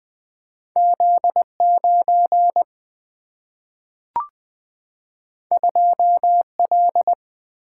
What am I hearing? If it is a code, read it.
Morse code, heard as a keyed tone, about 20 wpm, 700 Hz. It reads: Z 9 2 L
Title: Single-Letter Single-Number 20wpm